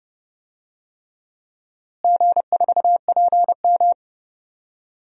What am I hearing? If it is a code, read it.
Morse code, heard as a keyed tone, about 28 wpm, 700 Hz. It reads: G4PM
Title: DXCC-100 Call Signs - 4 Characters - Repeated 3x - 30wpm